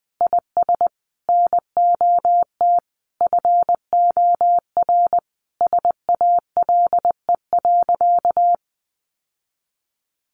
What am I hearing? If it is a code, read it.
Morse code, heard as a keyed tone, about 20 wpm, 700 Hz. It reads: IS NOT FOR SALE.
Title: Sentences from Top 1400 Words 20wpm